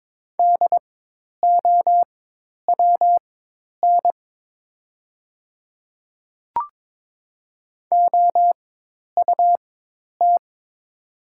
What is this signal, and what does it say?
Morse code, heard as a keyed tone, about 21 wpm, spaced as for 10 wpm, 700 Hz. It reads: DOWN OUT
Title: 4x Character Spacing - Top 100 Words - 22wpm